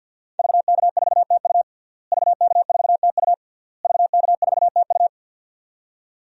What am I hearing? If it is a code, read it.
Morse code, heard as a keyed tone, about 46 wpm, 700 Hz. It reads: VK4TU VK4TU VK4TU
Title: DXCC-100 Call Signs - 5 Characters - Repeated 3x - 50wpm